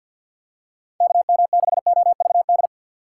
Morse code, heard as a keyed tone, about 53 wpm, 700 Hz. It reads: KM6OVD
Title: USA Call Signs - 3x Character Spacing - 50wpm